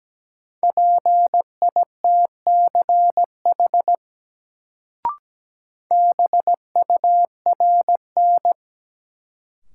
Morse code, heard as a keyed tone, about 17 wpm, 700 Hz. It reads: PITCH BURN
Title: Sentences from Top 1300 Words - New Word Drill - 17wpm